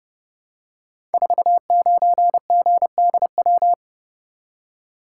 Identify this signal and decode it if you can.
Morse code, heard as a keyed tone, about 31 wpm, 700 Hz. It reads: 49GDW
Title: Code Groups - 5 Letter:Numbers - 30wpm